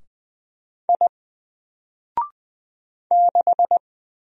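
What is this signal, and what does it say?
Morse code, heard as a keyed tone, about 20 wpm, 700 Hz. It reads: I 6
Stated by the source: Single letter and number Morse drill at 20 wpm, from N4LCD.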